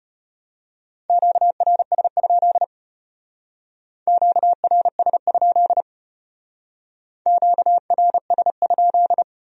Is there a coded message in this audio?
Yes: Morse code from 1.1 to 9.2 s, heard as a keyed tone, about 32 wpm, 700 Hz. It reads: QRS? QRS? QRS?